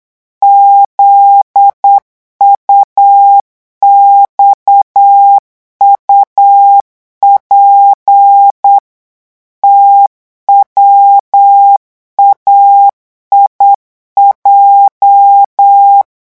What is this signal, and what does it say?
Morse code, heard as a keyed tone, about 8 wpm, 800 Hz. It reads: ZUXUP TWAIJ
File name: morse mp3 / revision8-vitesse 8 mots